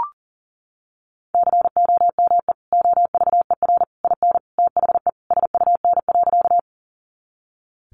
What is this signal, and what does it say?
Morse code, heard as a keyed tone, about 40 wpm, 700 Hz. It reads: COME OVER IN THE SUN.
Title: Sentences from Top 300 Words 40wpm